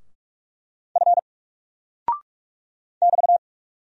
Morse code, heard as a keyed tone, about 44 wpm, 700 Hz. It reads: F <BT>
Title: Single Character (Letter:Number:Punctuation:Prosign) Alt 45wpm